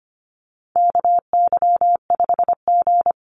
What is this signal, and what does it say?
Morse code, heard as a keyed tone, about 25 wpm, 700 Hz. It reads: KY5G
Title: USA Call Signs 25wpm